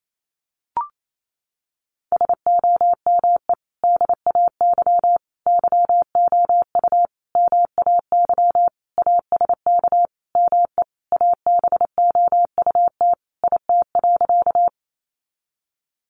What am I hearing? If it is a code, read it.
Morse code, heard as a keyed tone, about 28 wpm, 700 Hz. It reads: SOME DAY YOU MAY ASK ME ABOUT IT.
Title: Sentences from Top 200 Words 28wpm